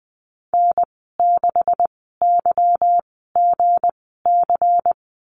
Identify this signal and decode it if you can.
Morse code, heard as a keyed tone, about 20 wpm, spaced as for 14 wpm, 700 Hz. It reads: N6YGC
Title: 161 Lesson 26g - Letter g - Call Signs 20wpm